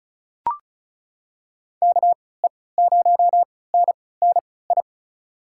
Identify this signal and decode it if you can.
Morse code, heard as a keyed tone, about 36 wpm, spaced as for 20 wpm, 700 Hz. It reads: KE0NNI
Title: USA Call Signs - 6 Characters - 3x Character Spacing - 35wpm